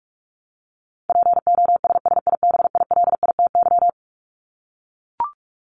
Morse code, heard as a keyed tone, about 44 wpm, 700 Hz. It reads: POSSIBILITY
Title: Top 1751-2000 Words 45wpm